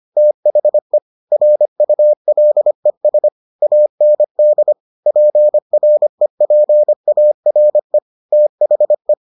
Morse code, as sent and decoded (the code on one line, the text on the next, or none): - .... / .-. ..- .-.. ... / .- -. -.. / .--. .-. .--. .- .-. / - .... .
THE RULES AND PREPARE THE